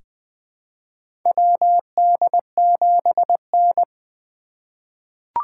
Morse code, heard as a keyed tone, about 21 wpm, 700 Hz. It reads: WD7N
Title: USA Call Signs - 4 Characters - 2x Character Spacing - 20wpm